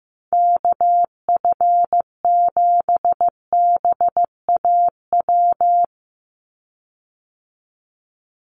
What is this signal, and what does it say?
Morse code, heard as a keyed tone, about 15 wpm, 700 Hz. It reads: KF7BAW
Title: USA Call Signs 15wpm